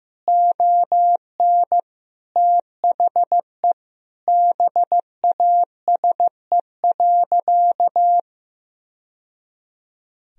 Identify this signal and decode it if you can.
Morse code, heard as a keyed tone, about 15 wpm, 700 Hz. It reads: ON THE BASE.
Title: Sentences from Top 1000 Words - Review Part 1 - 15wpm